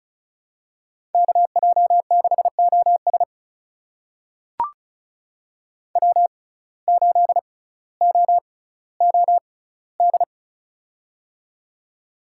Morse code, heard as a keyed tone, about 36 wpm, spaced as for 18 wpm, 700 Hz. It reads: KJ6OS W8OOD
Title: USA Call Signs - 5 Characters - 6x Character Spacing - 35wpm